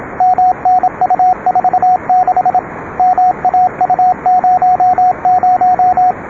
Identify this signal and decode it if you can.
Morse code, heard as a keyed tone, about 26 wpm, 780 Hz, with radio noise. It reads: MNU46 MAU00